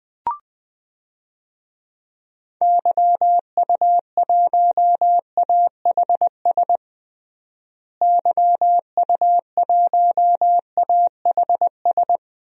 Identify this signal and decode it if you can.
Morse code, heard as a keyed tone, about 20 wpm, 700 Hz. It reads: YU1AHS YU1AHS
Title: DXCC-100 Call Signs - 6 Characters - Repeated 3x - 20wpm